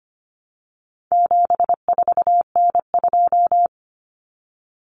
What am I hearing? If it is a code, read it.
Morse code, heard as a keyed tone, about 25 wpm, 700 Hz. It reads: 74N2
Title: Common Cut Numbers - 4-Digit 25wpm